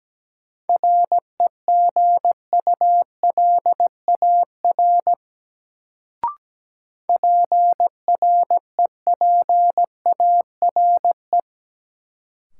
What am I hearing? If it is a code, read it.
Morse code, heard as a keyed tone, about 17 wpm, 700 Hz. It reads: REGULAR PREPARE
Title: Sentences from Top 900 Words - New Word Drill - 17wpm